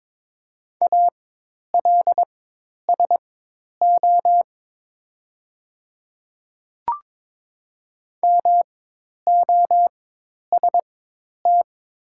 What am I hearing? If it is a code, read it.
Morse code, heard as a keyed tone, about 22 wpm, spaced as for 10 wpm, 700 Hz. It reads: ALSO MOST
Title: 4x Character Spacing - Top 200 Words - 22wpm